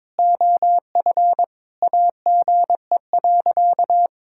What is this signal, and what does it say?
Morse code, heard as a keyed tone, about 22 wpm, 700 Hz. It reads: OF AGE.
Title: Sentences from Top 1000 Words - Review Part 6 - 22wpm